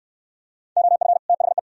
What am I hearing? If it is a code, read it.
Morse code, heard as a keyed tone, about 69 wpm, 700 Hz. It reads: OF THE